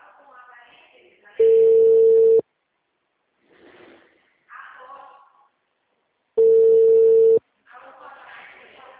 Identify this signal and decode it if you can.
Ringback tone (China, 450 Hz), with speech.
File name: out- -2041- - - .29418